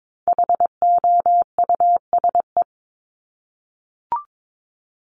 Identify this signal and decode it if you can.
Morse code, heard as a keyed tone, about 22 wpm, 700 Hz. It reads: HOUSE
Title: Top 200 Words Repeated 22wpm